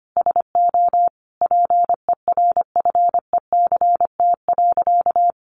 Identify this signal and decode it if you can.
Morse code, heard as a keyed tone, about 25 wpm, 700 Hz. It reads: SO PERFECT.
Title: Sentences from Top 1300 Words 25wpm